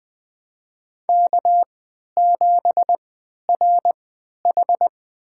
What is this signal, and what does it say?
Morse code, heard as a keyed tone, about 20 wpm, spaced as for 11 wpm, 700 Hz. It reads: K7RH
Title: USA Call Signs - 4 Characters - 3x Character Spacing - 20wpm